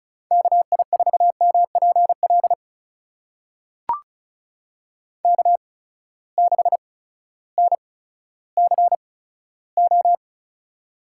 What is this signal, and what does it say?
Morse code, heard as a keyed tone, about 36 wpm, spaced as for 16 wpm, 700 Hz. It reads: KI4MPL K6NCO
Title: USA Call Signs - 8x Character Spacing - 35wpm